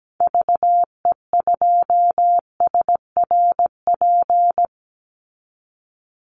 Morse code, heard as a keyed tone, about 17 wpm, 700 Hz. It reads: VE2SRP
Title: Call Signs - 17wpm